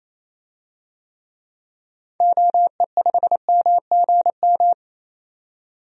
Morse code, heard as a keyed tone, about 27 wpm, 700 Hz. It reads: OE5MGM
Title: DXCC-100 Call Signs - 6 Characters - 28wpm